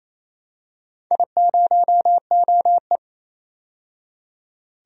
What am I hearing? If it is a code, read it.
Morse code, heard as a keyed tone, about 28 wpm, 700 Hz. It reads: I0OE
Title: DXCC-100 Call Signs - 4 Characters - 28wpm